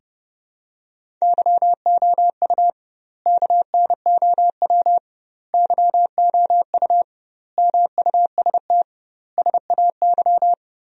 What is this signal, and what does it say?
Morse code, heard as a keyed tone, about 30 wpm, 700 Hz. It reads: YOU KNOW YOU MUST SAY
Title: 2x Word-Spacing - Sentences from Top 500 Words - 30wpm